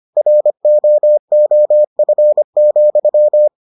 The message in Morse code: ROOF,

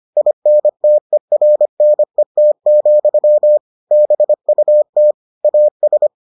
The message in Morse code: INTERNET, BUT AS

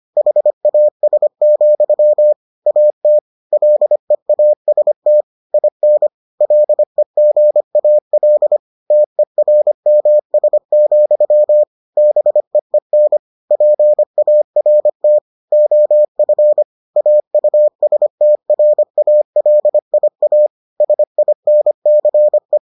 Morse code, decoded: HAS, AT LEAST IN LEGAL TERMS, BEEN PART OF AUSTRALIA SINCE

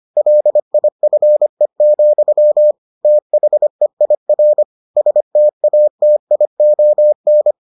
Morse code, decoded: LIFE, THEIR STATION

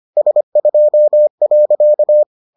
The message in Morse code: S2.

